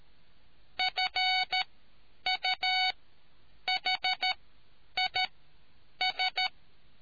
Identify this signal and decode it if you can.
Morse code, heard as a keyed tone, about 13 wpm, spaced as for 8 wpm, 2250 Hz. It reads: FUHIS